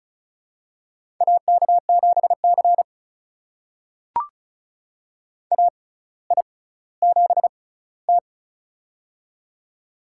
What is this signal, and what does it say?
Morse code, heard as a keyed tone, about 34 wpm, spaced as for 18 wpm, 700 Hz. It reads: AK7C AI7T
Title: USA Call Signs - 4 Characters - 6x Character Spacing - 35wpm